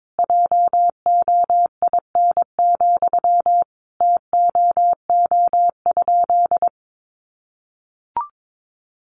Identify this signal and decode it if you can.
Morse code, heard as a keyed tone, about 22 wpm, 700 Hz. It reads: JOIN, TOO?
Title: Sentences from Top 1400 Words 22wpm